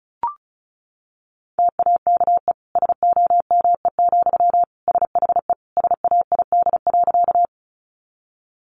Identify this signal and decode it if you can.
Morse code, heard as a keyed tone, about 34 wpm, 700 Hz. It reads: TAKE SOME, SHE SAID.
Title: Sentences from Top 200 Words 35wpm